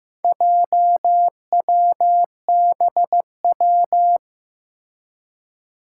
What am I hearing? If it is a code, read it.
Morse code, heard as a keyed tone, about 15 wpm, 700 Hz. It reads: JWBW
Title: Four Challenging Letter-Number Combinations 15wpm